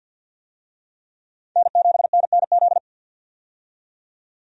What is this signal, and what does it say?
Morse code, heard as a keyed tone, about 52 wpm, 700 Hz. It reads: N7NNZ
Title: USA Call Signs - 5 Characters - 50wpm